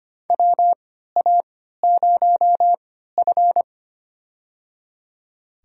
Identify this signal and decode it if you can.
Morse code, heard as a keyed tone, about 25 wpm, spaced as for 14 wpm, 700 Hz. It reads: WA0F
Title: USA Call Signs - 4 Characters - 3x Character Spacing - 25wpm